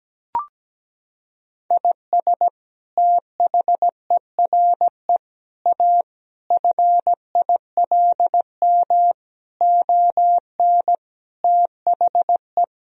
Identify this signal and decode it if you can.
Morse code, heard as a keyed tone, about 17 wpm, 700 Hz. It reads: IS THERE A FILM ON THE
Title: Sentences from Top 1000 Words - Review Part 3 - 17wpm